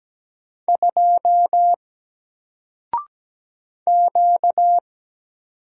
Morse code, heard as a keyed tone, about 17 wpm, 700 Hz. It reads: 2 Q
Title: Single Letter-Number Alt 17wpm